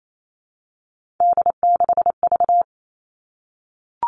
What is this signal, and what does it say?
Morse code, heard as a keyed tone, about 28 wpm, 700 Hz. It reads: D6V
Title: Three Challenging Letter-Number Combinations 28wpm